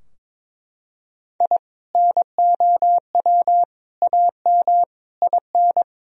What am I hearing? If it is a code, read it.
Morse code, heard as a keyed tone, about 22 wpm, 700 Hz. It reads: I NOW AM IN
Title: Sentences from Top 1000 Words - Review Part 2 - 22wpm